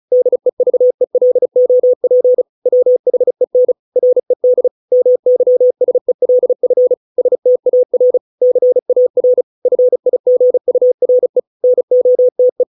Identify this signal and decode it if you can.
Morse code, heard as a keyed tone, about 35 wpm, 500 Hz. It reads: DEVELOP WHEN RED MYSELF STAR CAR FIGURE NOTE